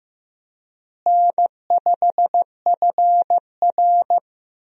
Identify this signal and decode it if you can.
Morse code, heard as a keyed tone, about 15 wpm, 700 Hz. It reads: N5FR
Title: USA Call Signs - 4 Characters - Repeated 2x - 15wpm